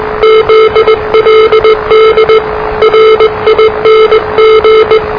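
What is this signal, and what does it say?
Morse code, heard as a keyed tone, about 20 wpm, 440 Hz, with noise. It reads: ZLD RING
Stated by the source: Morse challenge from NZ Net, the national amateur radio net of Aotearoa/New Zealand, which meets on 3535.0 kHz.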